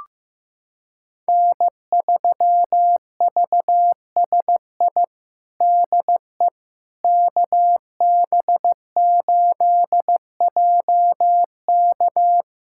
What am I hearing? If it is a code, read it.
Morse code, heard as a keyed tone, about 15 wpm, 700 Hz. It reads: N3VSI DE KB8JK
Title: USA Callsign DE Callsign - 5 Characters - 15wpm